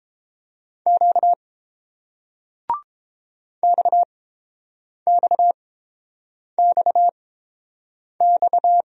Morse code, heard as a keyed tone, about 30 wpm, spaced as for 7 wpm, 700 Hz. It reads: Q XXXX